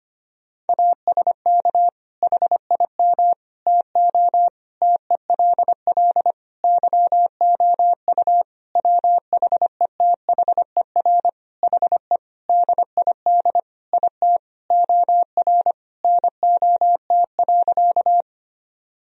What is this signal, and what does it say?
Morse code, heard as a keyed tone, about 25 wpm, 700 Hz. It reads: ASK HIM TO TELL YOU WHETHER HE DID IT OR NOT.
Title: Sentences from Top 1000 Words - Review Part 3 - 25wpm